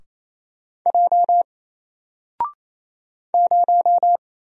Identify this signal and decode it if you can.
Morse code, heard as a keyed tone, about 28 wpm, 700 Hz. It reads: J 0
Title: Single Letter-Number Alt 28wpm